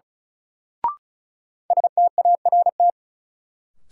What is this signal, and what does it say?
Morse code, heard as a keyed tone, about 35 wpm, 700 Hz. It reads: START
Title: Sentences from Top 300 Words - New Word Drill - 35wpm